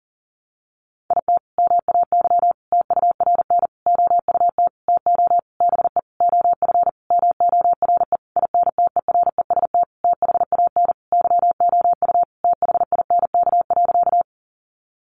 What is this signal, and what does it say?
Morse code, heard as a keyed tone, about 40 wpm, 700 Hz. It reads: IT MAY TURN OUT TO BE OF MORE INTEREST THAN YOU THINK.
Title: Sentences from Top 400 Words 40wpm